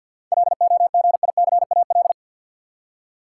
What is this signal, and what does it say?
Morse code, heard as a keyed tone, about 51 wpm, 700 Hz. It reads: LOGICAL
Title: Seven Letter Words - 50wpm